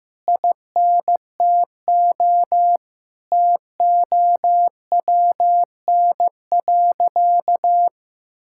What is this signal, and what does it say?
Morse code, heard as a keyed tone, about 15 wpm, 700 Hz. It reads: INTO TOWN.
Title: Sentences from Top 1000 Words - Review Part 6 - 15wpm